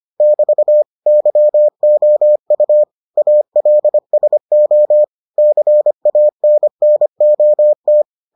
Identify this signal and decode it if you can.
Morse code, heard as a keyed tone, about 25 wpm, 600 Hz. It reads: <BT> YOU ALSO CANNOT